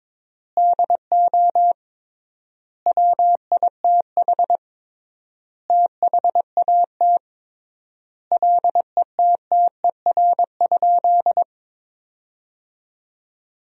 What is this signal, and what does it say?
Morse code, heard as a keyed tone, about 22 wpm, 700 Hz. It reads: DO WITH THAT LETTER?
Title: 3x Word-Spacing - Sentences from Top 300 Words - 22wpm